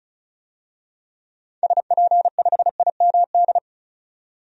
Morse code, heard as a keyed tone, about 34 wpm, 700 Hz. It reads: SP5IMD
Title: DXCC-100 Call Signs - 6 Characters - 35wpm